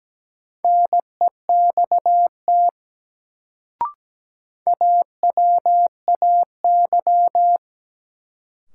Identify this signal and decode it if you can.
Morse code, heard as a keyed tone, about 17 wpm, 700 Hz. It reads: NEXT AWAY
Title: Sets of 2 Words - New Word Drill - 17wpm